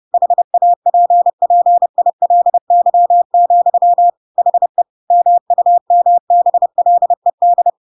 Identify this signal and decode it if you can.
Morse code, heard as a keyed tone, about 30 wpm, 700 Hz. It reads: HAPPILY, HE MUMBLED
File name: Pinocchio / Pinocchio ch1 30 30 ik1hge